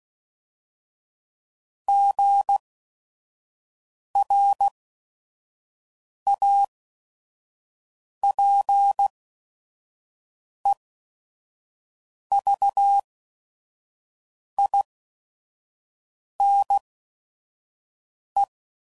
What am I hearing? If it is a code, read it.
Morse code, heard as a keyed tone, about 16 wpm, spaced as for 5 wpm, 800 Hz. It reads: GRAPEVINE
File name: CW Code Groups @5WPM / fromSM's / Cities